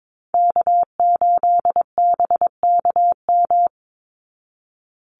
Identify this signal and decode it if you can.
Morse code, heard as a keyed tone, about 22 wpm, 700 Hz. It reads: K8BKM
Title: Call Signs Repeated 3x - 22wpm